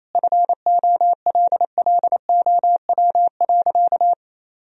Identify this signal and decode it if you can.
Morse code, heard as a keyed tone, about 28 wpm, 700 Hz. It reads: FOLLOW.
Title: Sentences from Top 1000 Words - Review Part 2 - 28wpm